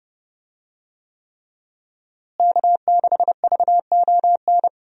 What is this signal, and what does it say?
Morse code, heard as a keyed tone, about 32 wpm, 700 Hz. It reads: K6VON
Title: USA Call Signs - 5 Characters - Repeated 3x - 30wpm